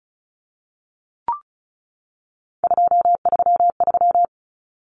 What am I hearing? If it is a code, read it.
Morse code, heard as a keyed tone, about 34 wpm, 700 Hz. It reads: 233
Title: Numbers - 3-Digit 35wpm